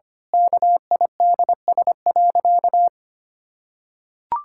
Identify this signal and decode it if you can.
Morse code, heard as a keyed tone, about 25 wpm, 700 Hz. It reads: KIDS.
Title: Pangram Challenge v2 - 25wpm